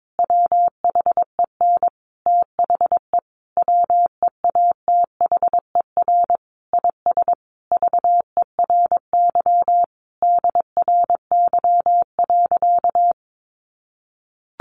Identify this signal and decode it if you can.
Morse code, heard as a keyed tone, about 22 wpm, 700 Hz. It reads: WHEN THE WEATHER IS VERY DRY.
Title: Sentences from Top 1400 Words 22wpm